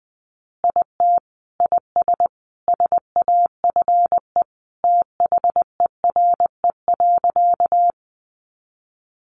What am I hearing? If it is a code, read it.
Morse code, heard as a keyed tone, about 20 wpm, 700 Hz. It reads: IT IS SAFE THERE.
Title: Easy Phrases 20wpm